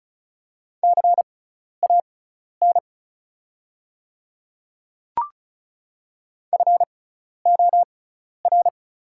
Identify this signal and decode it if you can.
Morse code, heard as a keyed tone, about 36 wpm, spaced as for 12 wpm, 690 Hz. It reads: CAN FOR
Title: 6x Character Spacing - Top 100 Words - 35wpm